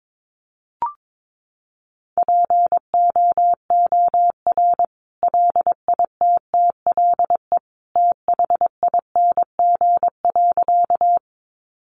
Morse code, heard as a keyed tone, about 22 wpm, 700 Hz. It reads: POOR LITTLE THING.
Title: Sentences from Top 1000 Words 22wpm